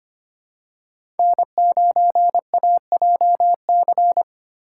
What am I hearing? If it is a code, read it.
Morse code, heard as a keyed tone, about 25 wpm, 700 Hz. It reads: N9AJC
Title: USA Call Signs - 2x Character Spacing - 25wpm